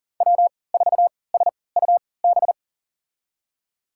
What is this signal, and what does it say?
Morse code, heard as a keyed tone, about 40 wpm, spaced as for 23 wpm, 700 Hz. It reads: W4SUB
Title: USA Call Signs - 3x Character Spacing - 40wpm